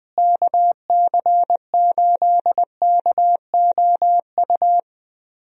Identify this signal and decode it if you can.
Morse code, heard as a keyed tone, about 20 wpm, 700 Hz. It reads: KC8KOU
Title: USA Call Signs - 6 Characters - Repeated 3x - 20wpm